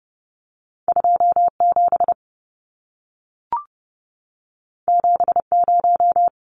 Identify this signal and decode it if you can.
Morse code, heard as a keyed tone, about 30 wpm, 700 Hz. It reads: 27 70
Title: Numbers - 2-Digit 30wpm